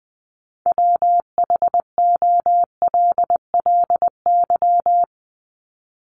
Morse code, heard as a keyed tone, about 20 wpm, 700 Hz. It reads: WHOLLY
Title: LY Suffix - Top 100 Words with Base Words 20wpm